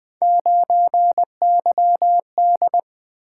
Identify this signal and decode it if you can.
Morse code, heard as a keyed tone, about 20 wpm, 700 Hz. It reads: 9YD
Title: Three Letter-Number Combinations - 20wpm